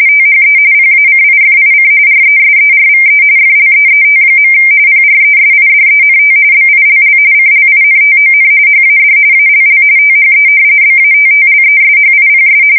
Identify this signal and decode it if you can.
RTTY, about 45.45 baud. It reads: THE PACIFIC RESERVE FLEET. PRESIDENT BILL CLINTON HAD ALL THE BATTLESHIPS
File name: fsk170 / battleshipnj